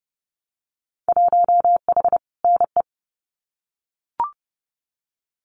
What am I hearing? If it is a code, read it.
Morse code, heard as a keyed tone, about 30 wpm, 700 Hz. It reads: 1H NE
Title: Winter Field Day Exchanges Encore 30wpm